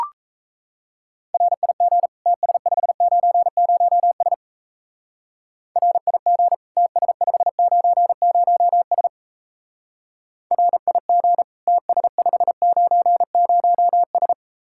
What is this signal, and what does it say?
Morse code, heard as a keyed tone, about 35 wpm, 700 Hz. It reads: RIG TS590S RIG TS590S RIG TS590S
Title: Speed-Racing Common QSO Phrases - 28wpm